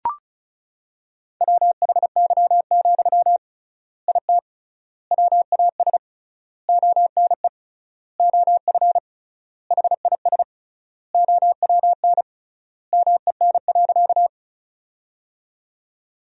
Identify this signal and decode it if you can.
Morse code, heard as a keyed tone, about 36 wpm, 700 Hz. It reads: WHY, IT WAS ONE OF HIS OWN MEN.
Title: 3x Word-Spacing - Sentences from Top 300 Words - 35wpm